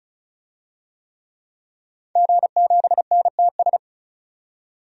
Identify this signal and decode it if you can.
Morse code, heard as a keyed tone, about 36 wpm, 700 Hz. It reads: G7NTS
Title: DXCC-100 Call Signs - 5 Characters - 35wpm